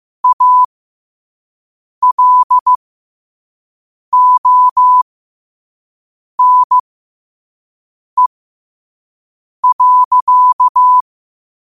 Morse code, heard as a keyed tone, about 15 wpm, spaced as for 5 wpm, 1000 Hz. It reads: ALONE.
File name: 5wpm